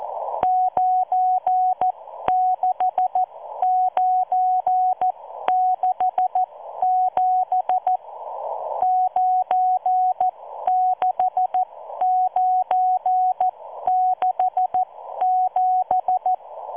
Morse code, heard as a keyed tone, about 14 wpm, spaced as for 12 wpm, 740 Hz. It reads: 96967 96967